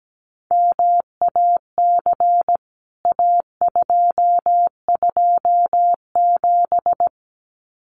Morse code, heard as a keyed tone, about 17 wpm, 700 Hz. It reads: MAC A227
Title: CWT Exchange 17wpm